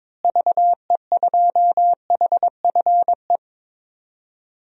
Morse code, heard as a keyed tone, about 22 wpm, 700 Hz. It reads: VE2HFE
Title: DXCC-100 Call Signs - 6 Characters - Repeated 2x - 22wpm